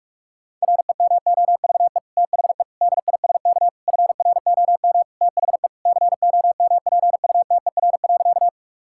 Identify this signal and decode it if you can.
Morse code, heard as a keyed tone, about 46 wpm, 700 Hz. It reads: REMOVE THE DISK FROM THE COMPUTER.
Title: Sentences from Top 1000 Words - Review Part 6 - 45wpm